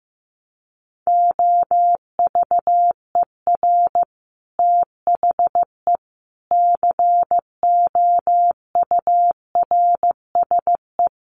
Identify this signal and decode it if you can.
Morse code, heard as a keyed tone, about 15 wpm, 700 Hz. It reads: OVER THE COURSE
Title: Sets of 4 Words 15wpm